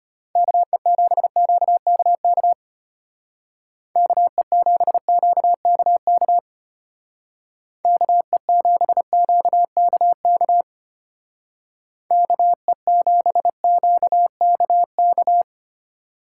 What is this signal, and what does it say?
Morse code, heard as a keyed tone, about 31 wpm, 700 Hz. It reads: KE7QKK KE7QKK KE7QKK KE7QKK
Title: Speed Racing - USA 2x3 Call Signs 25wpm